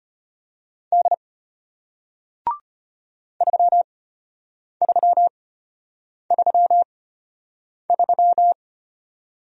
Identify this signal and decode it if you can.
Morse code, heard as a keyed tone, about 33 wpm, spaced as for 8 wpm, 700 Hz. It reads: D 3333